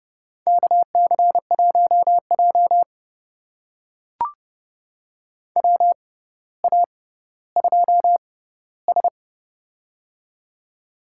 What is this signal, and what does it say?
Morse code, heard as a keyed tone, about 30 wpm, spaced as for 15 wpm, 700 Hz. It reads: KC1J WA2S
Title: USA Call Signs - 4 Characters - 6x Character Spacing - 30wpm